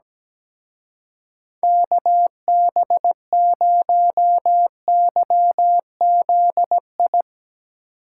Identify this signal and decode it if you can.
Morse code, heard as a keyed tone, about 19 wpm, 700 Hz. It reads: KB0YZI
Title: Speed Racing - USA 2x3 Call Signs 17wpm